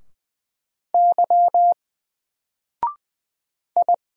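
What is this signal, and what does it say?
Morse code, heard as a keyed tone, about 20 wpm, 700 Hz. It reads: Y I
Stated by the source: Single letter and number Morse drill at 20 wpm, from N4LCD.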